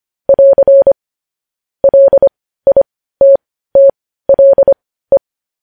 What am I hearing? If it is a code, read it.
Morse code, heard as a keyed tone, about 25 wpm, spaced as for 15 wpm, 550 Hz. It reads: <AR> LITTLE
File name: podcast / News headlines-1x-25x15